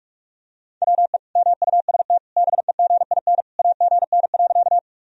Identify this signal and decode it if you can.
Morse code, heard as a keyed tone, about 46 wpm, 700 Hz. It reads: WE MUST BEGIN AGN.
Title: Sentences from Top 300 Words and Abbreviations 45wpm